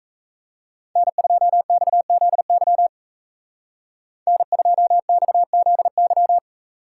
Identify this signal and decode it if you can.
Morse code, heard as a keyed tone, about 35 wpm, 700 Hz. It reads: N2XZY N2XZY
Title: Speed Racing - USA 1x3 Call Signs 28wpm